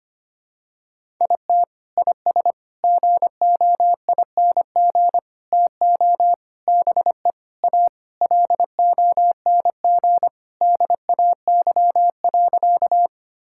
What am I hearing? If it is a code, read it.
Morse code, heard as a keyed tone, about 25 wpm, 700 Hz. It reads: IT IS GOING TO BE A LONG DAY.
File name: practice / 25wpm / Sentences with 4-Word Phrases - 25wpm